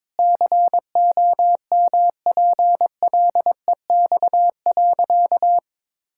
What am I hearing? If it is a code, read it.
Morse code, heard as a keyed tone, about 22 wpm, 700 Hz. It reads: COMPLEX.
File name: practice / 22wpm / Sentences with 4-Word Phrases - 22wpm